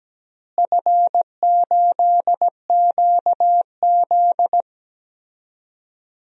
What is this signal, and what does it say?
Morse code, heard as a keyed tone, about 17 wpm, 700 Hz. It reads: F8QZ
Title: DXCC-100 Call Signs - 4 Characters - Repeated 2x - 17wpm